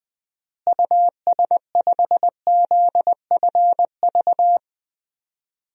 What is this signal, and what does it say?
Morse code, heard as a keyed tone, about 20 wpm, 700 Hz. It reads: US5ZFV
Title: DXCC-100 Call Signs - 6 Characters - Repeated 3x - 20wpm